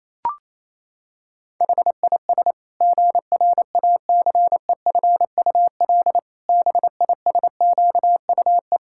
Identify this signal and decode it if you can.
Morse code, heard as a keyed tone, about 28 wpm, 700 Hz. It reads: HIS GRACEFUL BISQUE
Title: Pangram Challenge v2 - 28wpm